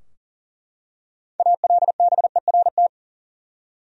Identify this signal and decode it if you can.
Morse code, heard as a keyed tone, about 40 wpm, 700 Hz. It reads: ALBERT
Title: USA Common Names - 40wpm